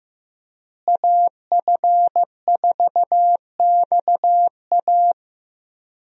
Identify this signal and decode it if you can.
Morse code, heard as a keyed tone, about 16 wpm, 700 Hz. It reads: AF4XA
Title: USA Call Signs - 5 Characters - Repeated 2x - 15wpm